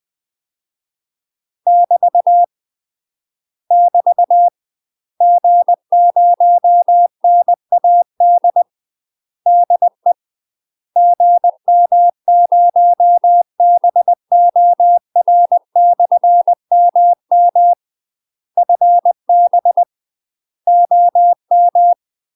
Morse code, heard as a keyed tone, about 20 wpm, 700 Hz. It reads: <BT> <BT> G0NAD DE GM0BOR/MM FB OM
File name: UK MASTER QSOs TRAINING / 20WPM / UK MASTER QSOs 0062